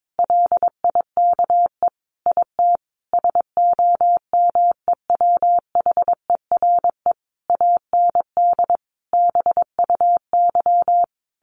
Morse code, heard as a keyed tone, about 22 wpm, 700 Hz. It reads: LIKE IT SOMEWHERE AND BUY
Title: Sentences from Top 1400 Words 22wpm